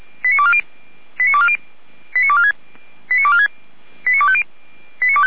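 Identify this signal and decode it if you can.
Five-tone selective call (ZVEI), heard as quick runs of five tones, tones of about 80 ms. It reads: 89240 89246 89246 89250
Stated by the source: ZVEI five-tone call received by SDR, from feuerwehr-orlamuende.org fire brigade audio archive.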